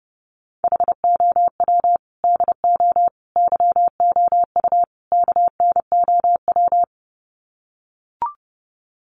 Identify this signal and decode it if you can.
Morse code, heard as a keyed tone, about 30 wpm, 700 Hz. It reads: HOW DO YOU KNOW
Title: Sets of 4 Words 30wpm